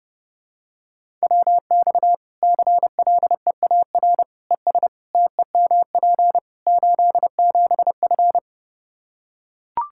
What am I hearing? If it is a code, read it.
Morse code, heard as a keyed tone, about 30 wpm, 700 Hz. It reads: WX CLEAR ES TEMP 87F
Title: Practice QSO - Exchange 2 - Info - 30wpm